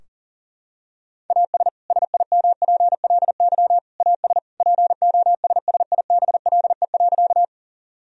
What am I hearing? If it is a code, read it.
Morse code, heard as a keyed tone, about 37 wpm, 700 Hz. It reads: AS SIMPLY AS POSSIBLE.
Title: Sentences from Top 1000 Words - Review Part 4 - 40wpm